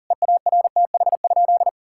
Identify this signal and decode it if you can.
Morse code, heard as a keyed tone, about 40 wpm, 700 Hz. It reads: EARTH?
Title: Sentences from Top 1600 Words 40wpm